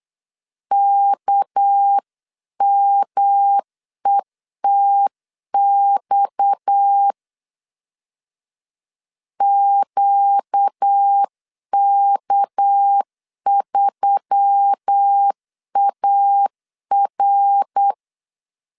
Morse code, heard as a keyed tone, about 8 wpm, 800 Hz. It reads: KMETX QK3AR
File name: CW / DET / 1 6 1500C S C 8wpm 5219